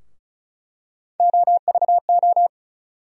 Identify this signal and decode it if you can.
Morse code, heard as a keyed tone, about 34 wpm, 700 Hz. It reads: OVO
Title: USA Call Sign Postfixes - 3 Letters 35wpm